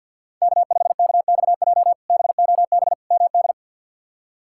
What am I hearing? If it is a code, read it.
Morse code, heard as a keyed tone, about 47 wpm, 700 Hz. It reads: K5KXJ BOB MD
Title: K1USN SST QSO Practice - 50wpm